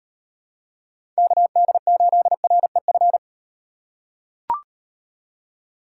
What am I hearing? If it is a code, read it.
Morse code, heard as a keyed tone, about 38 wpm, 700 Hz. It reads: KD8REF